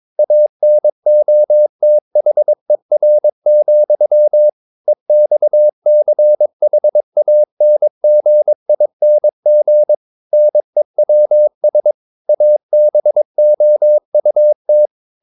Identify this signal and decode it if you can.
Morse code, heard as a keyed tone, about 22 wpm, 600 Hz. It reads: ANOTHER, EXCHANGING NEWS ABOUT
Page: Romanen Never af Ken Follet på engelsk.